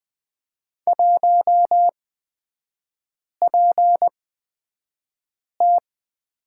Morse code, heard as a keyed tone, about 20 wpm, spaced as for 5 wpm, 700 Hz. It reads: 1PT
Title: 250 Lesson 40e - BK - Three Characters 20wpm